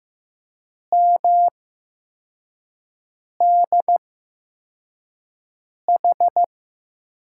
Morse code, heard as a keyed tone, about 15 wpm, spaced as for 4 wpm, 700 Hz. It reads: MDH
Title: USA Call Signs - 8x Character Spacing - 15wpm